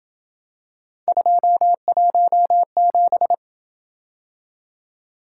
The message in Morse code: 217

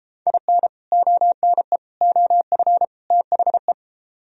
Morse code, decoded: IN ONE OF THE

